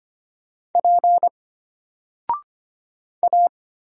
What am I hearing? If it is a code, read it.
Morse code, heard as a keyed tone, about 25 wpm, 700 Hz. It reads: P A
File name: Single-Letter-Number-25wpm